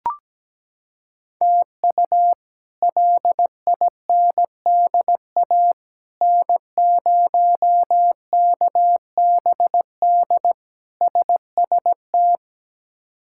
Morse code, heard as a keyed tone, about 17 wpm, 700 Hz. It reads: TU LINDA N0KBD SST
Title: K1USN SST QSO Practice - 17wpm